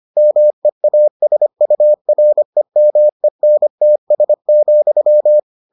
Morse code, heard as a keyed tone, about 25 wpm, 600 Hz. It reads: MEASUREMENTS,